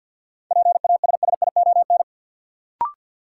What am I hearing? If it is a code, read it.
Morse code, heard as a keyed tone, about 53 wpm, 700 Hz. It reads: PASSION
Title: Sentences from Top 1300 Words - New Word Drill Alt - 50wpm